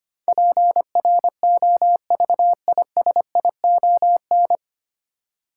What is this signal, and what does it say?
Morse code, heard as a keyed tone, about 25 wpm, 700 Hz. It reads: PROVISION
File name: practice / 25wpm / 9 Letter Words - First 3 Letters Given - 25wpm